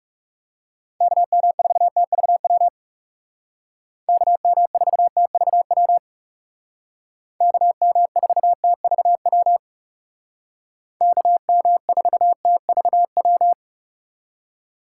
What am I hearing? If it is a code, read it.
Morse code, heard as a keyed tone, about 35 wpm, 690 Hz. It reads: KM4TVW KM4TVW KM4TVW KM4TVW
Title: Speed Racing - USA Call Signs 30wpm